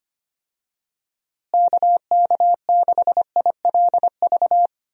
Morse code, heard as a keyed tone, about 25 wpm, 700 Hz. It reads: KK6ILV
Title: USA Call Signs - 6 Characters - Repeated 2x - 25wpm